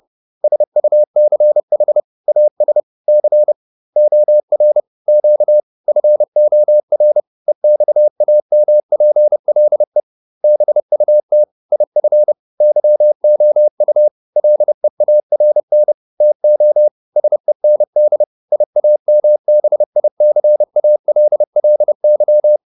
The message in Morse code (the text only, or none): SUCH AS C OR Q FOR EXAMPLE BUT IF YOU LEARN TO SEND IAMBICALLY